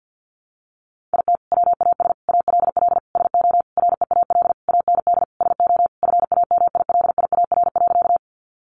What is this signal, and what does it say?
Morse code, heard as a keyed tone, about 50 wpm, 700 Hz. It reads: IT WAS ALL SO REAL AND SO FAMILIAR.
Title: Sentences from Top 1000 Words 50wpm